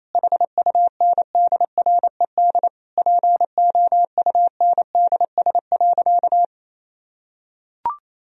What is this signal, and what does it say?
Morse code, heard as a keyed tone, about 28 wpm, 700 Hz. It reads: HUNDRED POUNDS.
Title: Sentences from Top 1000 Words - Review Part 2 - 28wpm